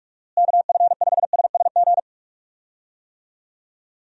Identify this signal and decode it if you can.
Morse code, heard as a keyed tone, about 46 wpm, 700 Hz. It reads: KF5SSG
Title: USA Call Signs - 6 Characters - Repeated 3x - 45wpm